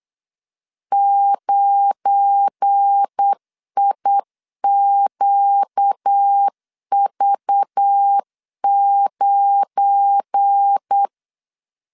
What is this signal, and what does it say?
Morse code, heard as a keyed tone, about 8 wpm, 800 Hz. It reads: 9IQV9
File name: LEZIONI / 7 GF9Z / 7 GF9 8wpm 3545